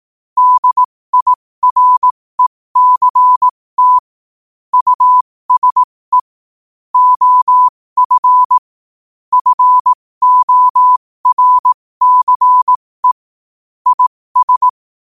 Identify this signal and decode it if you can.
Morse code, heard as a keyed tone, about 18 wpm, spaced as for 15 wpm, 1000 Hz. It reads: DIRECT USE OF FORCE IS
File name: qotdfiles / 15wpm